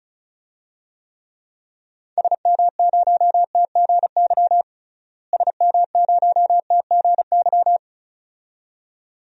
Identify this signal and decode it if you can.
Morse code, heard as a keyed tone, about 36 wpm, 700 Hz. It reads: SM0TGY SM0TGY
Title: DXCC-100 Call Signs - 6 Characters - Repeated 2x - 35wpm